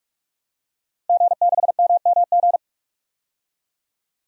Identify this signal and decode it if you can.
Morse code, heard as a keyed tone, about 46 wpm, 700 Hz. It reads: G6MMG
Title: DXCC-100 Call Signs - 5 Characters - Repeated 3x - 45wpm